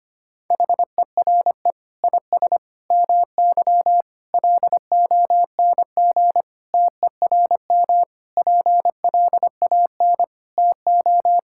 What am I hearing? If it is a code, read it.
Morse code, heard as a keyed tone, about 25 wpm, 700 Hz. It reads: HERE IS MY LONG TERM PLAN TO
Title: Sentences from Top 1600 Words 25wpm